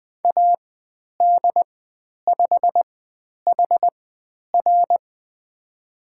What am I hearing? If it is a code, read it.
Morse code, heard as a keyed tone, about 20 wpm, spaced as for 10 wpm, 700 Hz. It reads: AD5HR
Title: 095 Lesson 16g - Letter c - Call Signs 20wpm